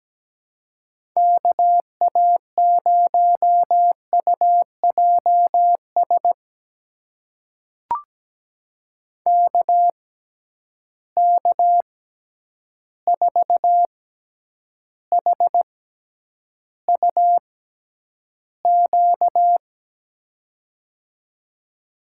Morse code, heard as a keyed tone, about 17 wpm, spaced as for 9 wpm, 700 Hz. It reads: KA0UJS KK4HUQ
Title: USA Call Signs - 6 Characters - 6x Character Spacing - 17wpm